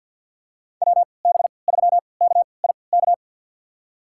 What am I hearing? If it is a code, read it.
Morse code, heard as a keyed tone, about 52 wpm, spaced as for 28 wpm, 700 Hz. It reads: WB3KIK
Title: USA Call Signs - 6 Characters - 3x Character Spacing - 50wpm